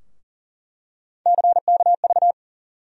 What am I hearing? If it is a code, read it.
Morse code, heard as a keyed tone, about 40 wpm, 700 Hz. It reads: CKV
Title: USA Call Sign Postfixes - 3 Letters 40wpm